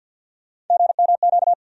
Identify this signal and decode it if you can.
Morse code, heard as a keyed tone, about 47 wpm, 700 Hz. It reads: GMQ
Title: USA Call Sign Postfixes - 3 Letters 50wpm